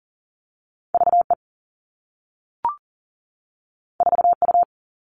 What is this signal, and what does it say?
Morse code, heard as a keyed tone, about 40 wpm, 700 Hz. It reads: V E 4 U
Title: Two Challenging Letter-Number Combinations 40wpm